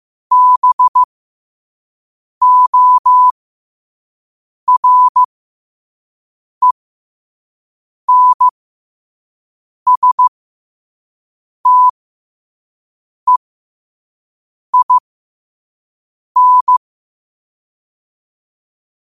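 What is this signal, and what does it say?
Morse code, heard as a keyed tone, about 15 wpm, spaced as for 5 wpm, 1000 Hz. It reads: BORENSTEIN
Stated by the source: Morse code sent at 5 wpm.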